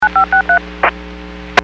DTMF, heard as telephone touch tones, tones of about 100 ms. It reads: 9563
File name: WPS / 01479 12-40 WPS 451.25 WPS